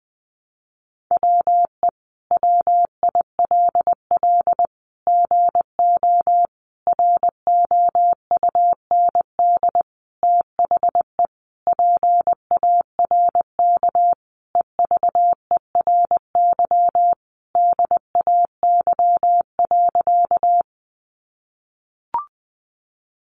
Morse code, heard as a keyed tone, about 20 wpm, 700 Hz. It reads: WE WILL GO ROUND THE PARK EVERY DAY.
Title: Sentences from Top 1300 Words 20wpm